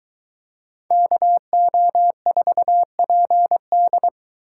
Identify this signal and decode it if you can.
Morse code, heard as a keyed tone, about 23 wpm, 700 Hz. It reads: KO4PD